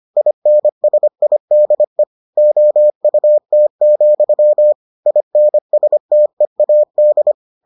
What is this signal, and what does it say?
Morse code, heard as a keyed tone, about 25 wpm, 600 Hz. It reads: INSIDE OUT, INSTEAD